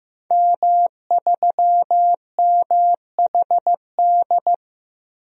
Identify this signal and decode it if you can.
Morse code, heard as a keyed tone, about 15 wpm, 700 Hz. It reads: M3MHD
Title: DXCC-100 Call Signs - 5 Characters - Repeated 3x - 15wpm